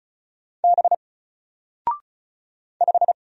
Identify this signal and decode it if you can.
Morse code, heard as a keyed tone, about 34 wpm, 700 Hz. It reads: B 5
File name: Single-Letter-Number-35wpm